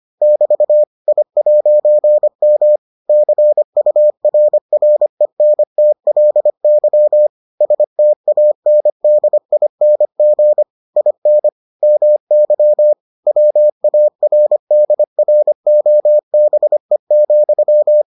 Morse code, heard as a keyed tone, about 25 wpm, 600 Hz. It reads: <BT> I'M CURRENTLY STANDING IN MY WARDROBE,